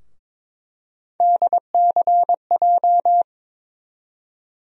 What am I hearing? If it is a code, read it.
Morse code, heard as a keyed tone, about 22 wpm, 700 Hz. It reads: DCJ
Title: USA Call Sign Postfixes - 3 Letters 22wpm